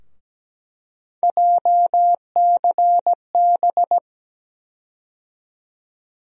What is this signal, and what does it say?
Morse code, heard as a keyed tone, about 17 wpm, 700 Hz. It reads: JCB
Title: USA Call Sign Postfixes - 3 Letters 17wpm